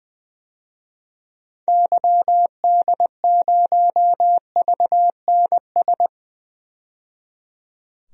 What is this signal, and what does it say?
Morse code, heard as a keyed tone, about 20 wpm, 700 Hz. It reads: YD0VNS
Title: DXCC-100 Call Signs - 6 Characters - 20wpm